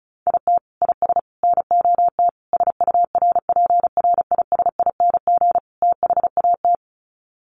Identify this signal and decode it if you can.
Morse code, heard as a keyed tone, about 34 wpm, 700 Hz. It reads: IT IS NOT SURPRISING THAT
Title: Sets of 5 Words 35wpm